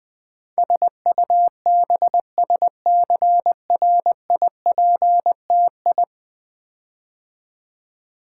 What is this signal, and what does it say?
Morse code, heard as a keyed tone, about 20 wpm, 700 Hz. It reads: SUBSCRIPTI
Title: Build A Word - 5 Letters At A Time - 12-Letter Words - 20wpm